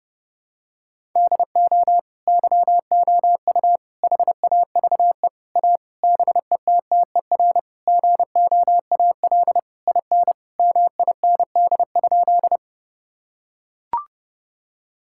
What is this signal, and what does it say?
Morse code, heard as a keyed tone, about 32 wpm, 700 Hz. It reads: DO YOU HAVE A BETTER GOAL IN MIND?
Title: Sentences from Top 1000 Words - Review Part 5 - 30wpm